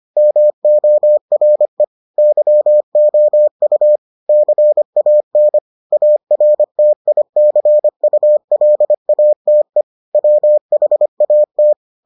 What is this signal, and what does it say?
Morse code, heard as a keyed tone, about 25 wpm, 600 Hz. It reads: MORE YOU CAN ARTICULATE WHAT